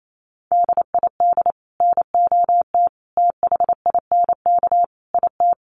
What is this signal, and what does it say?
Morse code, heard as a keyed tone, about 28 wpm, 700 Hz. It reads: DID NOT THINK IT
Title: Sets of 5 Words Encore 28wpm